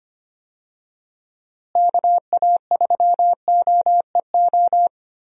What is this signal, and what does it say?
Morse code, heard as a keyed tone, about 25 wpm, 700 Hz. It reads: KA3OEO
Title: USA Call Signs - 6 Characters - Repeated 2x - 25wpm